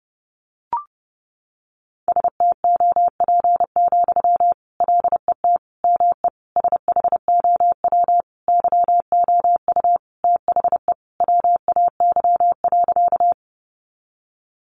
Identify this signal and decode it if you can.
Morse code, heard as a keyed tone, about 30 wpm, 700 Hz. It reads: STOP, LET ME SHOW YOU THE WAY.
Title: Sentences from Top 300 Words 30wpm